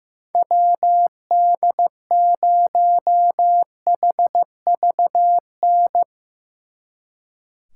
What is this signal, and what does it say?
Morse code, heard as a keyed tone, about 15 wpm, 700 Hz. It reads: WD0HVN
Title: Speed Racing - USA Call Signs 15wpm